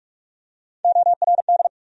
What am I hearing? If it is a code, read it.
Morse code, heard as a keyed tone, about 45 wpm, 700 Hz. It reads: ORD
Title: USA Call Sign Postfixes - 3 Letters 45wpm